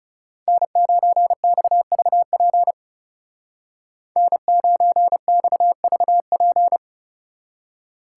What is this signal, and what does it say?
Morse code, heard as a keyed tone, about 35 wpm, 700 Hz. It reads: N9XVP N9XVP
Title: Speed Racing - USA Call Signs 30wpm